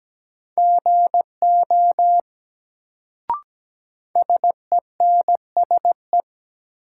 Morse code, heard as a keyed tone, about 17 wpm, 700 Hz. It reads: GO SENSE
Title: Sets of 2 Words - New Word Drill - 17wpm